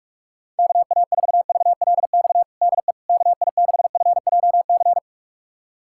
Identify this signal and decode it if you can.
Morse code, heard as a keyed tone, about 46 wpm, 690 Hz. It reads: KA4VLX DE KI6FJC
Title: USA Callsign DE Callsign - 6 Characters - 45wpm